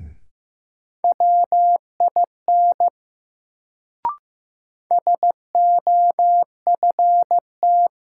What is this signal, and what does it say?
Morse code, heard as a keyed tone, about 15 wpm, 700 Hz. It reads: WIN SOFT
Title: Sentences from Top 1500 Words - New Word Drill - 15wpm